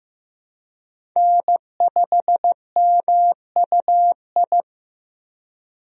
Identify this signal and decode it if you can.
Morse code, heard as a keyed tone, about 15 wpm, 700 Hz. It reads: N5MUI
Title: USA Call Signs - 5 Characters - Repeated 2x - 15wpm